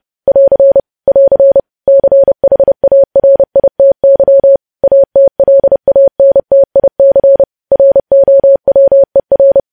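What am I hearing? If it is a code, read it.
Morse code, heard as a keyed tone, about 30 wpm, 550 Hz. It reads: <AR> <AR> CHARITY ATLANTIC ROWER
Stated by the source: News headlines sent in Morse code at 30.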